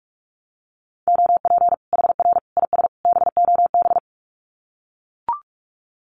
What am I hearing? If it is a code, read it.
Morse code, heard as a keyed tone, about 44 wpm, 700 Hz. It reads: OP HR IS BOB
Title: Common QSO Phrases 45wpm